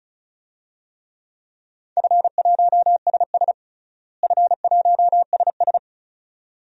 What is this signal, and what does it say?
Morse code, heard as a keyed tone, about 36 wpm, 700 Hz. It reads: F1SS F1SS
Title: DXCC-100 Call Signs - 4 Characters - Repeated 2x - 35wpm